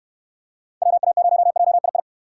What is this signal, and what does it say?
Morse code, heard as a keyed tone, about 69 wpm, 700 Hz. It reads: WA0JII